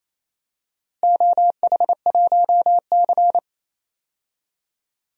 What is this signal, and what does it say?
Morse code, heard as a keyed tone, about 28 wpm, 700 Hz. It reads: OH1C
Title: DXCC-100 Call Signs - 4 Characters - Repeated 3x - 28wpm